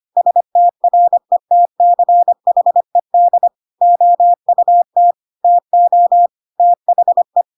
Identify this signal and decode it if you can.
Morse code, heard as a keyed tone, about 25 wpm, 700 Hz. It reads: STRETCHED OUT TO THE
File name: Pinocchio / Pinocchio ch5 25 25 ik1hge